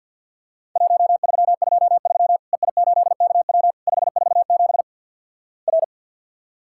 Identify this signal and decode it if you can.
Morse code, heard as a keyed tone, about 50 wpm, 700 Hz. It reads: 1323 EI8KW 547 R
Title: Worked All Europe DX Contest - QTC Exchange - 50wpm